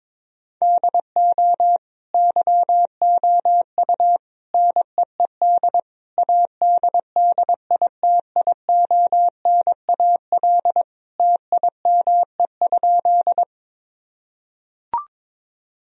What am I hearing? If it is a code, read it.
Morse code, heard as a keyed tone, about 22 wpm, 700 Hz. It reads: DO YOU NEED ADDITIONAL TIME?
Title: Sentences from Top 1000 Words - Review Part 4 - 22wpm